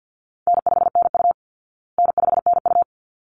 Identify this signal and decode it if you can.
Morse code, heard as a keyed tone, about 49 wpm, 700 Hz. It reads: N5NU N5NU
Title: Call Signs Repeated 3x - 50wpm